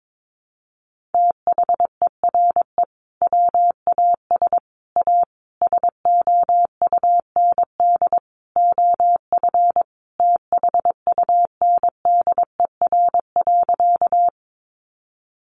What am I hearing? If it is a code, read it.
Morse code, heard as a keyed tone, about 22 wpm, 700 Hz. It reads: THERE WAS A SOUND OF THUNDER.
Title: A Sound Of Thunder 22wpm